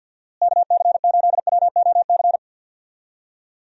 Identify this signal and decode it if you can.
Morse code, heard as a keyed tone, about 51 wpm, 700 Hz. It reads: KC8WOC
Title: USA Call Signs - 6 Characters - Repeated 2x - 50wpm